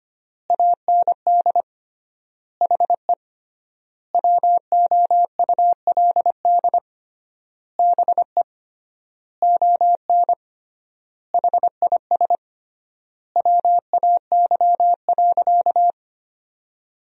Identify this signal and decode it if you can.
Morse code, heard as a keyed tone, about 25 wpm, 700 Hz. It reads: AND HE WOULD BE ON HIS WAY.
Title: 3x Word-Spacing - Sentences from Top 300 Words - 25wpm